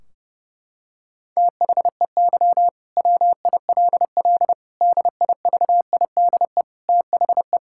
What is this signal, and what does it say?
Morse code, heard as a keyed tone, about 32 wpm, 700 Hz. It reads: THEY WILL DIVIDE THE
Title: Sentences from Top 1600 Words 30wpm